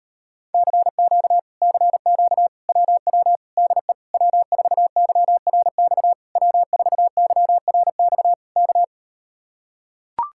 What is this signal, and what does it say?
Morse code, heard as a keyed tone, about 38 wpm, 700 Hz. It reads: CQ CQ WW DE W4YRX W4YRX K